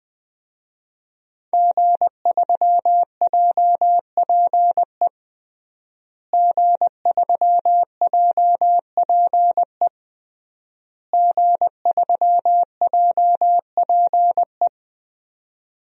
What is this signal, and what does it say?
Morse code, heard as a keyed tone, about 21 wpm, 700 Hz. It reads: G3JPE G3JPE G3JPE
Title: DXCC-100 Call Signs - 5 Characters - Repeated 3x - 20wpm